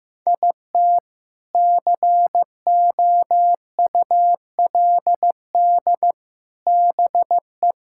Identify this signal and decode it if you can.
Morse code, heard as a keyed tone, about 16 wpm, 700 Hz. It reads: IT COULD BE
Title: Sentences from Top 1000 Words - Review Part 5 - 15wpm